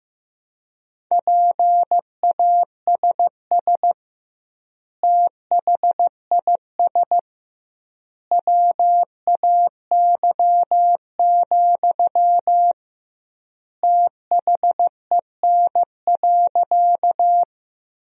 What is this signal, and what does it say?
Morse code, heard as a keyed tone, about 15 wpm, 700 Hz. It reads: PASS THIS WAY, THEN.
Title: 2x Word-Spacing - Sentences from Top 400 Words - 15wpm